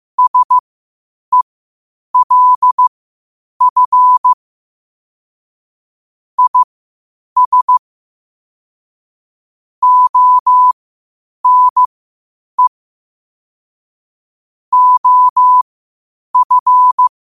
Morse code, heard as a keyed tone, about 15 wpm, spaced as for 8 wpm, 1000 Hz. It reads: SELF IS ONE OF
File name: qotdfiles / 8wpm